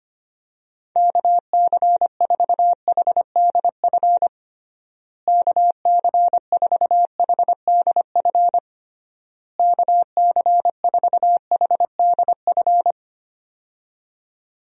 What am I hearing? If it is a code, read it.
Morse code, heard as a keyed tone, about 25 wpm, 700 Hz. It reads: KC4HDF KC4HDF KC4HDF
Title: USA Call Signs - 6 Characters - Repeated 3x - 25wpm